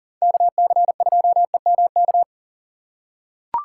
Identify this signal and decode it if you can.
Morse code, heard as a keyed tone, about 40 wpm, 700 Hz. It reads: KC2EMK
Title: USA Call Signs - 6 Characters - 4x Character Spacing - 40wpm